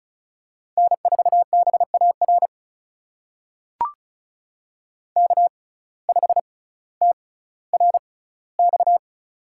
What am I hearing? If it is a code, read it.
Morse code, heard as a keyed tone, about 36 wpm, spaced as for 18 wpm, 700 Hz. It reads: N4BAR K5TRX
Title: USA Call Signs - 5 Characters - 6x Character Spacing - 35wpm